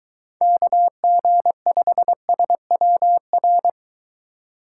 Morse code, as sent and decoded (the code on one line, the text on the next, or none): -.- --. ..... ... .-- .-.
KG5SWR